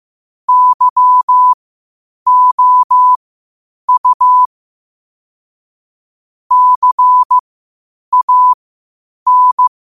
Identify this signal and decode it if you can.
Morse code, heard as a keyed tone, about 15 wpm, spaced as for 8 wpm, 1000 Hz. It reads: YOU CAN
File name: qotdfiles / 8wpm